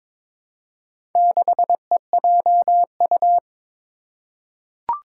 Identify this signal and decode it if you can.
Morse code, heard as a keyed tone, about 21 wpm, 700 Hz. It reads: 6EJU
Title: Four Challenging Letter-Number Combinations 22wpm